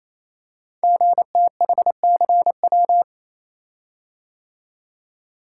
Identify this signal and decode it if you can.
Morse code, heard as a keyed tone, about 27 wpm, 700 Hz. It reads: GTHCW
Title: Code Groups - 5 Letters - 28wpm